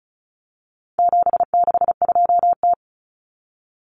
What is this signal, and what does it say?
Morse code, heard as a keyed tone, about 34 wpm, 700 Hz. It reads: 762T
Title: Common Cut Numbers - 4-Digit 35wpm